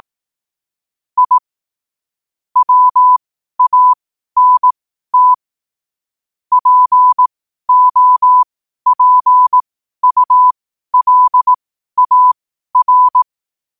Morse code, heard as a keyed tone, about 18 wpm, spaced as for 12 wpm, 1000 Hz. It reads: I WANT POPULAR
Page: Quotes for Wed, 13 Aug 2025 in Morse Code at 12 words per minute.